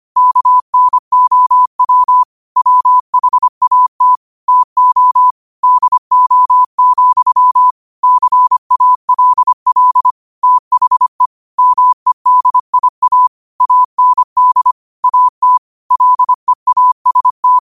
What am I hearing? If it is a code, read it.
Morse code, heard as a keyed tone, about 25 wpm, 1000 Hz. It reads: KNOW WHAT TO DO, CALL THE MEDIA AND AT LEAST